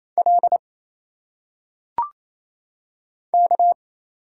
Morse code, heard as a keyed tone, about 27 wpm, 700 Hz. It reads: L K
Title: Single Letters - Phonetic Pronunciation 28wpm